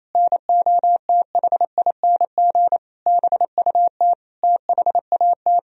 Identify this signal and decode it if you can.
Morse code, heard as a keyed tone, about 28 wpm, 700 Hz. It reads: NOTHING BUT THAT
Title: Sentences from Top 1000 Words - Review Part 1 - 28wpm